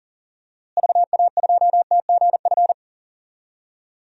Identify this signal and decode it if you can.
Morse code, heard as a keyed tone, about 40 wpm, 700 Hz. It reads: VA2TGF
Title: DXCC-100 Call Signs - 6 Characters - Repeated 3x - 40wpm